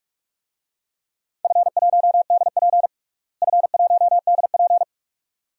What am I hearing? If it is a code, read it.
Morse code, heard as a keyed tone, about 46 wpm, 700 Hz. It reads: F1DP F1DP
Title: DXCC-100 Call Signs - 4 Characters - Repeated 2x - 45wpm